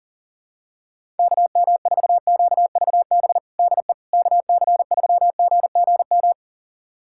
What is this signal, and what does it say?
Morse code, heard as a keyed tone, about 40 wpm, 700 Hz. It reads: KM4QVB DE KC3GGM
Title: USA Callsign DE Callsign - 6 Characters - 40wpm